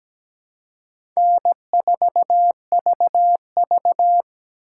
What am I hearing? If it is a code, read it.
Morse code, heard as a keyed tone, about 17 wpm, 700 Hz. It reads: N4VV
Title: USA Call Signs - 4 Characters - 6x Character Spacing - 17wpm